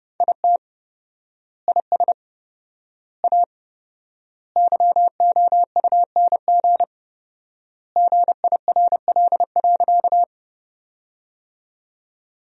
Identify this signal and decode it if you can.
Morse code, heard as a keyed tone, about 30 wpm, 700 Hz. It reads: IT IS A YOUNG GIRL.
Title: 4x Word-Spacing - Sentences from Top 400 Words - 30wpm